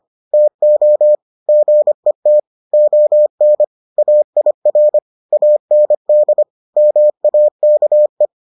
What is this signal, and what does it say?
Morse code, heard as a keyed tone, about 25 wpm, 600 Hz. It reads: TO GET ON AIR AND MAKE